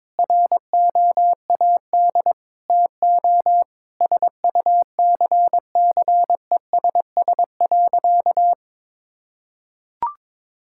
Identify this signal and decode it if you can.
Morse code, heard as a keyed tone, about 21 wpm, 700 Hz. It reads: ROAD TO SUCCESS.
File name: Difficult Sentences with Hints - 22wpm